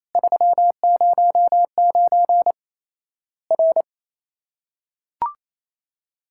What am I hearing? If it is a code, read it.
Morse code, heard as a keyed tone, about 28 wpm, 700 Hz. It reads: 309 R
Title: Worked All Europe DX Contest - QTC Exchange - 28wpm